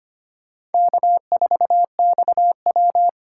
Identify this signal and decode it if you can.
Morse code, heard as a keyed tone, about 32 wpm, 700 Hz. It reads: K4XW